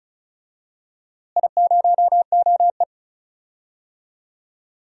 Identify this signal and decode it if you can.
Morse code, heard as a keyed tone, about 36 wpm, 700 Hz. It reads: I0OE
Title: DXCC-100 Call Signs - 4 Characters - Repeated 2x - 35wpm